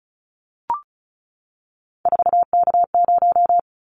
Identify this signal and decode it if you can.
Morse code, heard as a keyed tone, about 36 wpm, 700 Hz. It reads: 4K0
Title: Callsign Prefixes for Top 100 DXCC - 35wpm